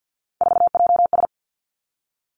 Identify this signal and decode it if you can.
Morse code, heard as a keyed tone, about 50 wpm, 700 Hz. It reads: 4JS
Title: Three Challenging Letter-Number Combinations 50wpm